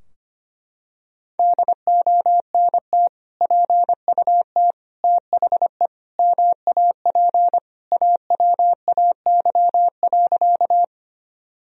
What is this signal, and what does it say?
Morse code, heard as a keyed tone, about 25 wpm, 700 Hz. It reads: DONT PUT THE MAP AWAY.
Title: Sentences from Top 1000 Words - Review Part 2 - 25wpm